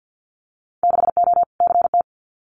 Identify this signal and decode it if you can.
Morse code, heard as a keyed tone, about 50 wpm, 700 Hz. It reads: 6O CT
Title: Winter Field Day Exchanges Encore 50wpm